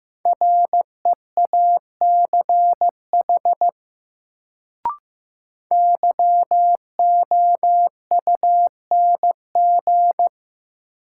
Morse code, heard as a keyed tone, about 15 wpm, 700 Hz. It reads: REACH YOUNG
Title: Sentences from Top 400 Words - New Word Drill - 15wpm